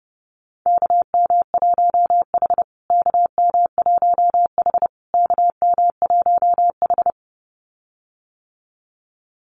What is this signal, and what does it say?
Morse code, heard as a keyed tone, about 30 wpm, 700 Hz. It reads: KM1H KM1H KM1H
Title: USA Call Signs Repeated 3x 30wpm